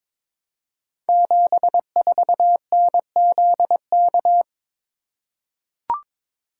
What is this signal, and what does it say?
Morse code, heard as a keyed tone, about 22 wpm, 700 Hz. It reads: 74NZK
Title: Code Groups - 5 Letter:Numbers - 22wpm